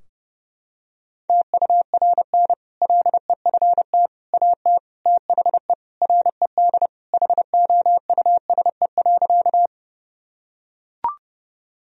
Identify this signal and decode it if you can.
Morse code, heard as a keyed tone, about 30 wpm, 700 Hz. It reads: TURN LEFT AT THE RED HOUSE.
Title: Sentences from Top 1000 Words - Review Part 2 - 30wpm